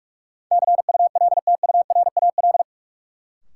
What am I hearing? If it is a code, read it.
Morse code, heard as a keyed tone, about 46 wpm, 700 Hz. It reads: CULTURAL
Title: Sentences from Top 1500 Words - New Word Drill - 45wpm